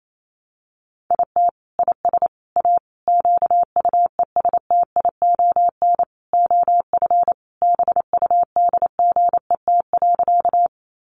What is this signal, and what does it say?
Morse code, heard as a keyed tone, about 28 wpm, 700 Hz. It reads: IT IS A QUESTION OF BUDGET.
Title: Sentences from Top 1200 Words 28wpm